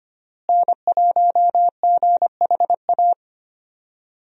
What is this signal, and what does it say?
Morse code, heard as a keyed tone, about 25 wpm, 700 Hz. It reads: N1GHA
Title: USA Call Signs - 5 Characters - Repeated 2x - 25wpm